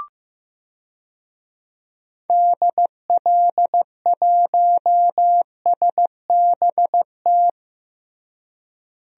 Morse code, heard as a keyed tone, about 15 wpm, 700 Hz. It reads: DL1SBT
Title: DXCC-100 Call Signs - 6 Characters - Repeated 3x - 15wpm